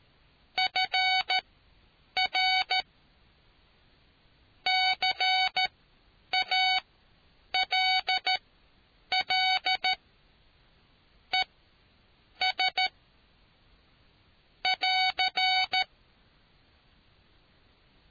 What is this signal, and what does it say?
Morse code, heard as a keyed tone, about 14 wpm, spaced as for 8 wpm, 750 Hz. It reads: FR CALL ES <AR>